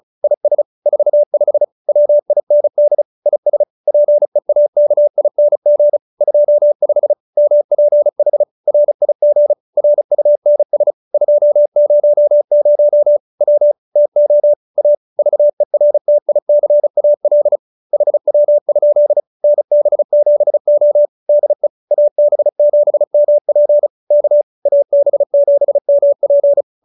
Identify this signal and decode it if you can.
Morse code, heard as a keyed tone, about 34 wpm, 590 Hz. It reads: IS 45 WIND IS PEAKING 25 MPH RIG RUNS 200 W TO A VERTICAL HW? NB7O DE AB7MP K AB7MP